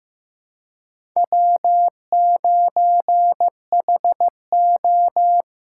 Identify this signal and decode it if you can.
Morse code, heard as a keyed tone, about 15 wpm, 700 Hz. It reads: W9HO
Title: USA Call Signs - 4 Characters - 2x Character Spacing - 15wpm